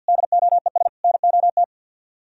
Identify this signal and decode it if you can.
Morse code, heard as a keyed tone, about 63 wpm, 700 Hz. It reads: DOES NOT